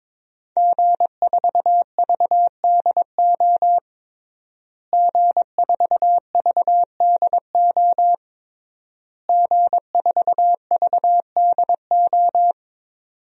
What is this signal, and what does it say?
Morse code, heard as a keyed tone, about 22 wpm, 700 Hz. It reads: G4VDO G4VDO G4VDO
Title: DXCC-100 Call Signs - 5 Characters - Repeated 3x - 22wpm